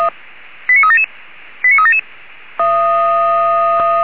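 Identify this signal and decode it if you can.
Five-tone selective call (ZVEI), heard as quick runs of five tones, tones of about 70 ms. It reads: 89377 89377
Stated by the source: ZVEI five-tone selective call 89379 received by SDR.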